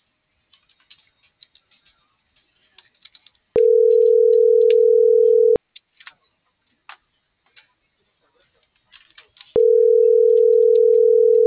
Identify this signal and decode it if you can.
Ringback tone (North America), 440 + 480 Hz.